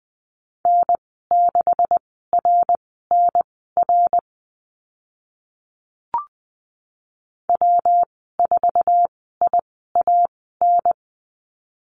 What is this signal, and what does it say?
Morse code, heard as a keyed tone, about 20 wpm, spaced as for 14 wpm, 700 Hz. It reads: N6RNR W4IAN
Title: 161 Lesson 26g - Letter g - Call Signs 20wpm